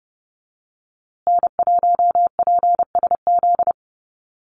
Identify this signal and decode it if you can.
Morse code, heard as a keyed tone, about 28 wpm, 700 Hz. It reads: N1PSZ
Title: USA Call Signs Repeated 3x 30wpm